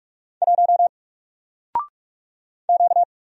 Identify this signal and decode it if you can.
Morse code, heard as a keyed tone, about 44 wpm, 700 Hz. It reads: 1 Q
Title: Single Letter-Number Alt 45wpm